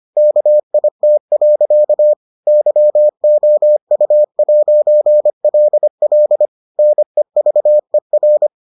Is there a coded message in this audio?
Yes: Morse code from 0.2 to 8.5 s, heard as a keyed tone, about 25 wpm, 600 Hz. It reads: KIT. YOU'LL NEVER